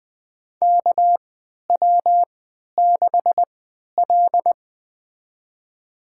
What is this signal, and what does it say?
Morse code, heard as a keyed tone, about 20 wpm, spaced as for 11 wpm, 700 Hz. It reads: KW6L
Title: USA Call Signs - 3x Character Spacing - 20wpm